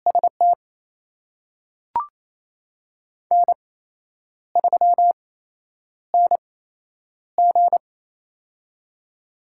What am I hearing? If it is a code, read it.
Morse code, heard as a keyed tone, about 28 wpm, spaced as for 9 wpm, 700 Hz. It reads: ST N3NG
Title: USA Call Signs - 4 Characters - 8x Character Spacing - 28wpm